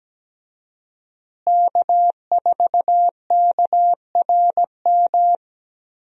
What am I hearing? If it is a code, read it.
Morse code, heard as a keyed tone, about 17 wpm, 700 Hz. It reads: K4KRM
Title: USA Call Signs - 5 Characters - Repeated 3x - 17wpm